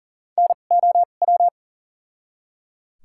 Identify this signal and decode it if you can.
Morse code, heard as a keyed tone, about 39 wpm, spaced as for 29 wpm, 700 Hz. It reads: NOW
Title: 2x Character Spacing - Top 300 Words - 40wpm